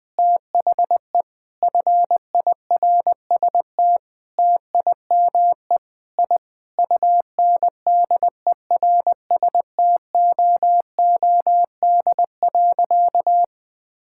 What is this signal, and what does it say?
Morse code, heard as a keyed tone, about 20 wpm, 700 Hz. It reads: THE FIRST TIME I UNDERSTOOD.
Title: Sentences with 5-Word Phrases - 20wpm